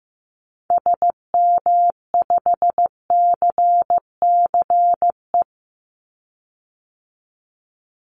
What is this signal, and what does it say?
Morse code, heard as a keyed tone, about 15 wpm, 700 Hz. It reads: SM5CCE
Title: Call Signs Repeated 3x - 15wpm